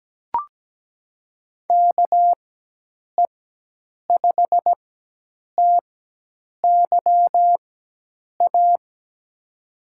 Morse code, heard as a keyed tone, about 17 wpm, spaced as for 8 wpm, 700 Hz. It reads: KE5TYA
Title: USA Call Signs - 4x Character Spacing - 17wpm